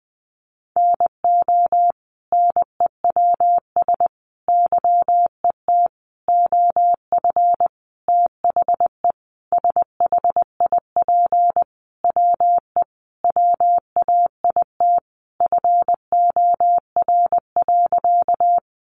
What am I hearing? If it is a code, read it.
Morse code, heard as a keyed tone, about 20 wpm, 700 Hz. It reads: NO NEWS YET OF THE SHIP WE WAIT FOR.
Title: Sentences from Top 800 Words 20wpm